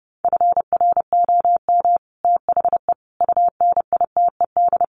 Morse code, heard as a keyed tone, about 30 wpm, 700 Hz. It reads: FROM THE UNITED
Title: Sets of 4 Words 30wpm